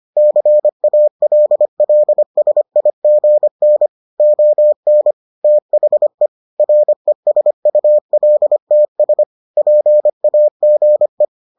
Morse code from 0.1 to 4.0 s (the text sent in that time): CALLSIGN